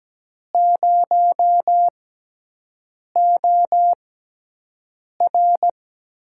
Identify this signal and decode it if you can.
Morse code, heard as a keyed tone, about 17 wpm, spaced as for 6 wpm, 700 Hz. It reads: 0OR
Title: USA Call Signs - 4 Characters - 6x Character Spacing - 17wpm